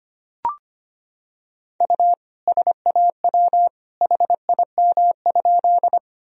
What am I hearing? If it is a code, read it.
Morse code, heard as a keyed tone, about 25 wpm, 700 Hz. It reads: U SAW HIM?
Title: Sentences from Top 300 Words and Abbreviations 25wpm